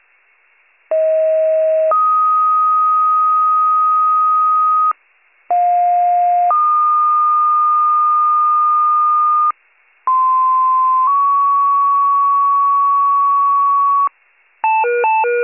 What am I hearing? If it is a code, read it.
Two-tone page, heard as a one-second tone, then a three-second tone, about 640 then 1250 Hz, about 710 then 1220 Hz, about 1040 then 1120 Hz. It is heard from 0.9 to 14.1 s.